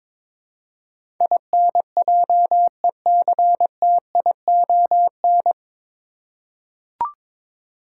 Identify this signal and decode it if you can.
Morse code, heard as a keyed tone, about 22 wpm, 700 Hz. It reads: INJECTION
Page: Morse Code Ninja practice 22wpm